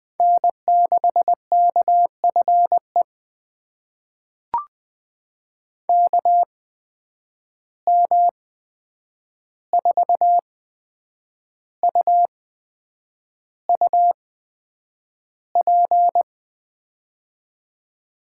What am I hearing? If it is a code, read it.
Morse code, heard as a keyed tone, about 20 wpm, spaced as for 8 wpm, 700 Hz. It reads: N6KFE KM4UUP
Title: USA Call Signs - 8x Character Spacing - 20wpm